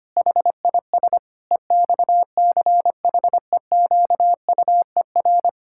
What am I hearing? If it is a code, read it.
Morse code, heard as a keyed tone, about 25 wpm, 700 Hz. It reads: HIS EXCHEQUER
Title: Pangram Challenge v2 - 25wpm